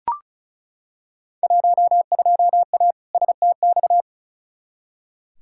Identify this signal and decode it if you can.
Morse code, heard as a keyed tone, about 36 wpm, 700 Hz. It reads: 12A STX
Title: Field Day Exchanges Encore - 35wpm